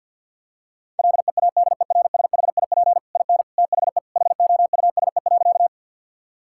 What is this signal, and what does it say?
Morse code, heard as a keyed tone, about 50 wpm, 700 Hz. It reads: LEADERSHIP IN THE HOUSE.
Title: Sentences from Top 1600 Words 50wpm